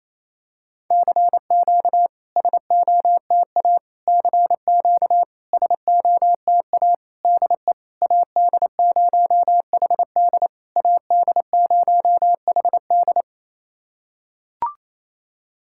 Morse code, heard as a keyed tone, about 28 wpm, 700 Hz. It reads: CQ SOTA CQ SOTA DE AD0HD AD0HD
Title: SOTA QSO Practice 28wpm